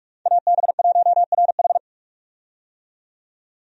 Morse code, heard as a keyed tone, about 46 wpm, 700 Hz. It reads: AB1RH
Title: USA Call Signs - 5 Characters - 45wpm